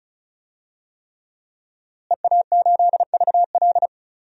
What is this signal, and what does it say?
Morse code, heard as a keyed tone, about 36 wpm, 700 Hz. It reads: EA8VL
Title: DXCC-100 Call Signs - 5 Characters - Repeated 3x - 35wpm